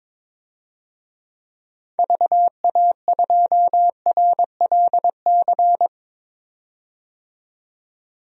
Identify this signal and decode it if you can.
Morse code, heard as a keyed tone, about 21 wpm, 700 Hz. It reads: VA2RLC
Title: DXCC-100 Call Signs - 6 Characters - 22wpm